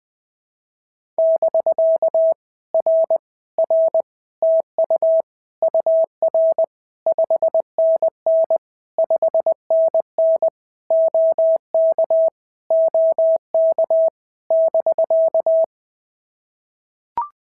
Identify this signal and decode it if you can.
Morse code, heard as a keyed tone, about 20 wpm, 650 Hz. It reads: <BK> R R TU UR 5NN 5NN OK OK <BK>
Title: POTA QSO Practice - 20wpm